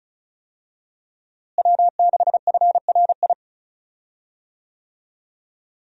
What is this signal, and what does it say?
Morse code, heard as a keyed tone, about 36 wpm, 700 Hz. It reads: W6FRI
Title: USA Call Signs - 5 Characters - Repeated 3x - 35wpm